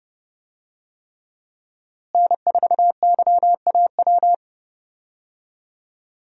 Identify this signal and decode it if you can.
Morse code, heard as a keyed tone, about 30 wpm, 700 Hz. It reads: N4YAW
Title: USA Call Signs - 5 Characters - Repeated 3x - 30wpm